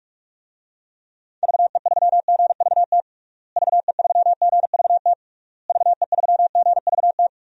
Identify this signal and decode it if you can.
Morse code, heard as a keyed tone, about 42 wpm, 700 Hz. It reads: VE3GVT VE3GVT VE3GVT
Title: DXCC-100 Call Signs - 6 Characters - Repeated 3x - 45wpm